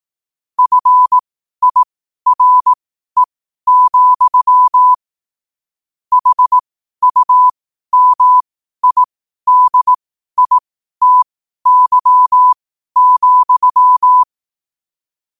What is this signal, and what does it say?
Morse code, heard as a keyed tone, about 18 wpm, spaced as for 12 wpm, 1000 Hz. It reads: FIRE, HUMIDITY,
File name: qotdfiles / 12wpm